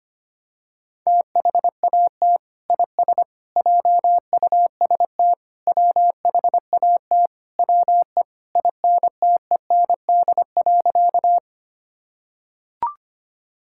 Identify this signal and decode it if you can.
Morse code, heard as a keyed tone, about 25 wpm, 700 Hz. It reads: THAT IS JUST WHAT WE INTEND.
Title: Sentences from Top 1600 Words 25wpm